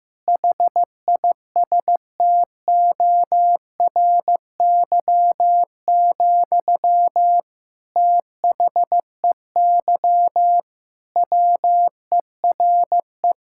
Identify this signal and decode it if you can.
Morse code, heard as a keyed tone, about 16 wpm, 700 Hz. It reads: HISTORY, THEY WERE
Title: Sentences from Top 1000 Words - Review Part 5 - 15wpm